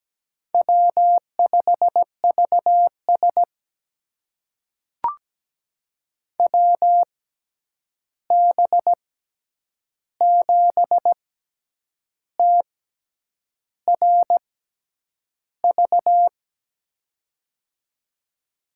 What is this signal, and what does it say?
Morse code, heard as a keyed tone, about 17 wpm, spaced as for 8 wpm, 700 Hz. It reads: W5VS WB7TRV
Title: USA Call Signs - 6x Character Spacing - 17wpm